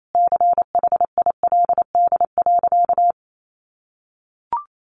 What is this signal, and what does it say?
Morse code, heard as a keyed tone, about 28 wpm, 700 Hz. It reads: CHILD.
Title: Sentences from Top 1300 Words 28wpm